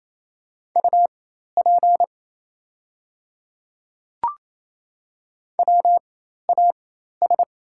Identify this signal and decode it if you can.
Morse code, heard as a keyed tone, about 28 wpm, spaced as for 13 wpm, 700 Hz. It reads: UP WAS
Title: 4x Character Spacing - Top 200 Words - 28wpm